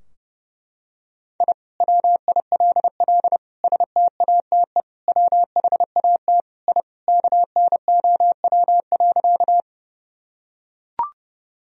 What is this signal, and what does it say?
Morse code, heard as a keyed tone, about 30 wpm, 700 Hz. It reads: I WILL STATE WHAT I KNOW.
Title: Sentences from Top 1000 Words - Review Part 2 - 30wpm